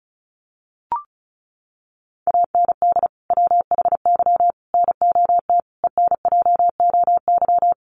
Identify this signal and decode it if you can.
Morse code, heard as a keyed tone, about 34 wpm, 700 Hz. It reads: AND WHY NOT ENJOY
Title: Sentences from Top 1300 Words 35wpm